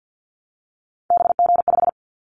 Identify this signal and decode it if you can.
Morse code, heard as a keyed tone, about 50 wpm, 700 Hz. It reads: BG5
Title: Callsign Prefixes for Top 100 DXCC - 50wpm